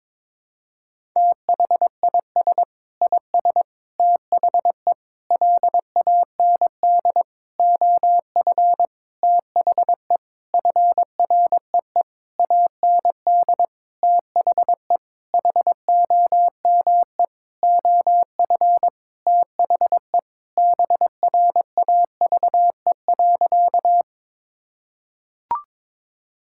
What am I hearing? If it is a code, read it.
Morse code, heard as a keyed tone, about 22 wpm, 700 Hz. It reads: THIS IS THE LAND OF THE FREE AND THE HOME OF THE BRAVE.
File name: Sentences with 9-Word Phrases - 22wpm